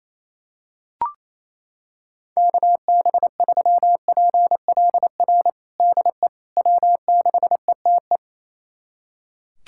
Morse code, heard as a keyed tone, about 28 wpm, 700 Hz. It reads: KB3PLR DE W6ETE
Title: USA Callsign DE Callsign 28wpm